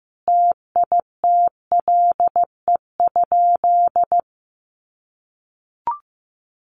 Morse code, heard as a keyed tone, about 15 wpm, 700 Hz. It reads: TITLE?
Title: Sentences from Top 1200 Words 15wpm